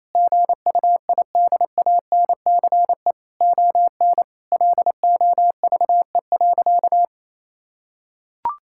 Morse code, heard as a keyed tone, about 27 wpm, 700 Hz. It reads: GUIDANCE ON LOVE.
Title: Sentences from Top 1600 Words 28wpm